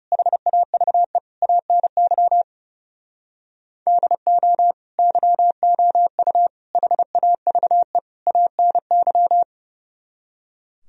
Morse code, heard as a keyed tone, about 38 wpm, 700 Hz. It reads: HAVE ANY DO YOU HAVE ANY